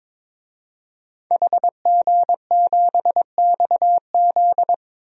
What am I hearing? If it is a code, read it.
Morse code, heard as a keyed tone, about 22 wpm, 700 Hz. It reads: HG7XZ
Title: DXCC-100 Call Signs - 5 Characters - 22wpm